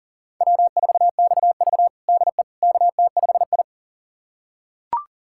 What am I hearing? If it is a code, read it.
Morse code, heard as a keyed tone, about 40 wpm, 700 Hz. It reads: W4XV DE KT5I
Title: USA Callsign DE Callsign - 4 Characters - 40wpm